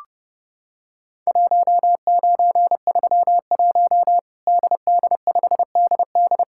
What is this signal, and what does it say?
Morse code, heard as a keyed tone, about 29 wpm, 700 Hz. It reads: 1931 DD5DD
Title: Worked All Europe DX Contest - QTC Exchange - 30wpm